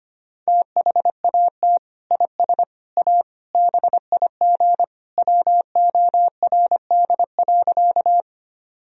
Morse code, heard as a keyed tone, about 25 wpm, 700 Hz. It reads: THAT IS A BIG WORD.
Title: Sentences from Top 1000 Words - Review Part 1 - 25wpm